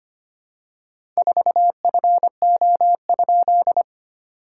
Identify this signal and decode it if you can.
Morse code, heard as a keyed tone, about 25 wpm, 700 Hz. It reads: 4FO?
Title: SOTA QSO Practice 25wpm